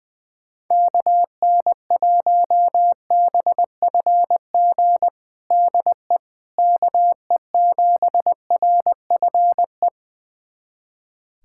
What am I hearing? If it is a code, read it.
Morse code, heard as a keyed tone, about 21 wpm, 700 Hz. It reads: KN1BFG DE KE7RFE
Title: USA Callsign DE Callsign - 6 Characters - 20wpm